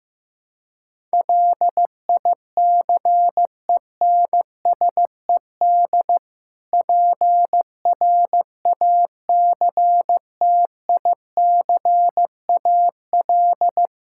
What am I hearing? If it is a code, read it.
Morse code, heard as a keyed tone, about 15 wpm, 700 Hz. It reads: LICENSED PRACTICAL
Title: Practice QSO - Exchange 3 - More Info - 15wpm